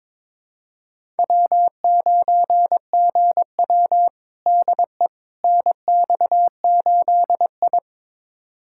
Morse code, heard as a keyed tone, about 22 wpm, 700 Hz. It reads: W9GW DE NX8I
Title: USA Callsign DE Callsign - 4 Characters - 22wpm